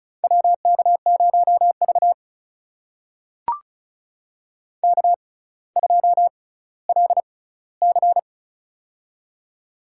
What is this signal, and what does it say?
Morse code, heard as a keyed tone, about 36 wpm, spaced as for 18 wpm, 700 Hz. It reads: WK0V K2LC
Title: USA Call Signs - 4 Characters - 6x Character Spacing - 35wpm